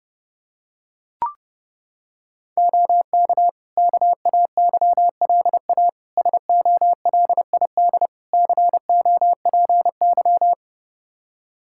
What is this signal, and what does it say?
Morse code, heard as a keyed tone, about 30 wpm, 700 Hz. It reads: OK KAYLA SOLID COPY
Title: K1USN SST QSO Practice 30wpm